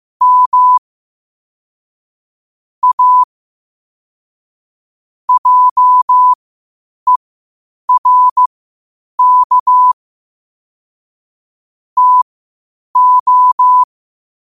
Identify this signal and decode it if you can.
Morse code, heard as a keyed tone, about 15 wpm, spaced as for 8 wpm, 1000 Hz. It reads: M A JERK TO
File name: qotdfiles / 8wpm